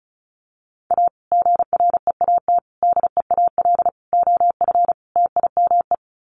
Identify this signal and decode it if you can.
Morse code, heard as a keyed tone, about 34 wpm, 700 Hz. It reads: A GREAT DEAL OF TIME
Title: Sets of 5 Words Encore2 35wpm